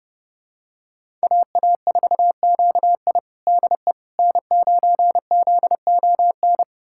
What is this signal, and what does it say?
Morse code, heard as a keyed tone, about 31 wpm, 700 Hz. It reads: AA4QI DE N9ZON
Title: USA Callsign DE Callsign - 5 Characters - 30wpm